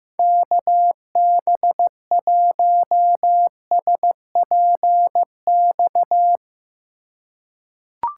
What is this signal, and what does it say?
Morse code, heard as a keyed tone, about 15 wpm, 700 Hz. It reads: KB1SPX
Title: USA Call Signs - 6 Characters - 3x Character Spacing - 15wpm